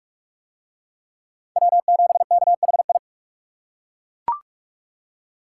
Morse code, heard as a keyed tone, about 40 wpm, 700 Hz. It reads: W7KHI